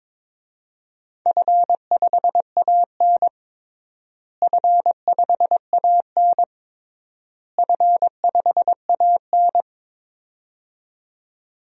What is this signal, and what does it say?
Morse code, heard as a keyed tone, about 22 wpm, 700 Hz. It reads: F5AN F5AN F5AN
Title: DXCC-100 Call Signs - 4 Characters - Repeated 3x - 22wpm